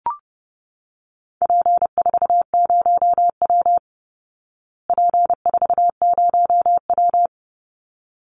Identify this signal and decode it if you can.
Morse code, heard as a keyed tone, about 30 wpm, 700 Hz. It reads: P40W P40W
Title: Call Signs Repeated 3x - 30wpm